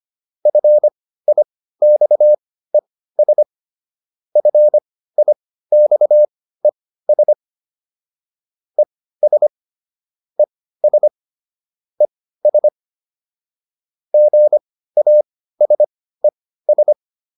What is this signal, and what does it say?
Morse code, heard as a keyed tone, about 25 wpm, spaced as for 15 wpm, 600 Hz. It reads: FIXES FIXES ES ES ES GASES